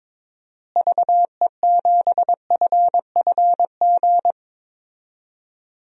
Morse code, heard as a keyed tone, about 22 wpm, 700 Hz. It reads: VE7FFG
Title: DXCC-100 Call Signs - 6 Characters - Repeated 2x - 22wpm